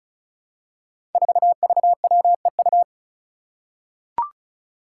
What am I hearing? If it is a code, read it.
Morse code, heard as a keyed tone, about 36 wpm, 700 Hz. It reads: 4VWEU
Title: Five Challenging Letter-Number Combinations 35wpm